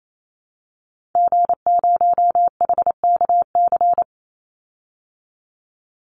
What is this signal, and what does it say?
Morse code, heard as a keyed tone, about 28 wpm, 700 Hz. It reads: G0HKC
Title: Call Signs Repeated 2x - 28wpm